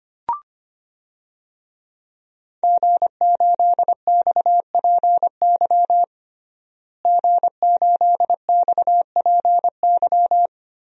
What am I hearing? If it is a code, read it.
Morse code, heard as a keyed tone, about 25 wpm, 700 Hz. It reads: G8XPY G8XPY
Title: DXCC-100 Call Signs - 5 Characters - Repeated 3x - 25wpm